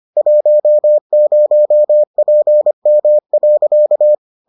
Morse code, heard as a keyed tone, about 25 wpm, 600 Hz. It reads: 10PM.